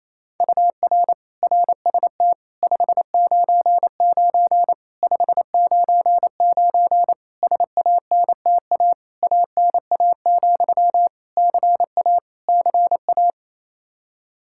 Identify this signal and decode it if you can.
Morse code, heard as a keyed tone, about 28 wpm, 700 Hz. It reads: UR RST 599 599 SANTA ANA, CA CA
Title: Practice QSO - Exchange 1 - Beginning - 28wpm